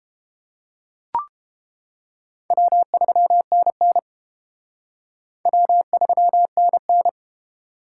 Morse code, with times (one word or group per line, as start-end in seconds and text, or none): 2.5-4.0 W3NN
5.5-7.1 W3NN